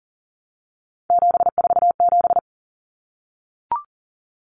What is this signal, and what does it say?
Morse code, heard as a keyed tone, about 40 wpm, 700 Hz. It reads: 747
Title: Common Cut Numbers - 3-Digit 40wpm